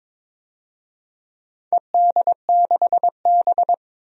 Morse code, heard as a keyed tone, about 21 wpm, 700 Hz. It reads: ED6B
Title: DXCC-100 Call Signs - 4 Characters - 22wpm